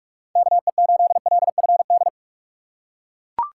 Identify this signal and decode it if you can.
Morse code, heard as a keyed tone, about 40 wpm, 700 Hz. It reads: KE8LFD